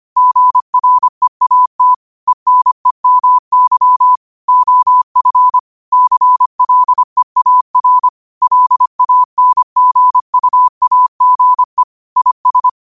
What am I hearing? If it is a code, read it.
Morse code, heard as a keyed tone, about 25 wpm, 1000 Hz. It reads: GREAT ENEMY OF CLEAR LANGUAGE IS